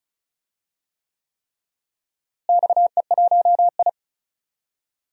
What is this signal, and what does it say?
Morse code, heard as a keyed tone, about 34 wpm, 700 Hz. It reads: XE1I
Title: DXCC-100 Call Signs - 4 Characters - Repeated 3x - 35wpm